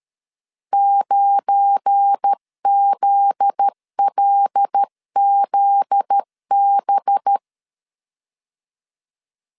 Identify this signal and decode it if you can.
Morse code, heard as a keyed tone, about 13 wpm, 800 Hz. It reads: 9ZLZB
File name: LEZIONI / 7 GF9Z / 7 GF9Z 12wpm 2147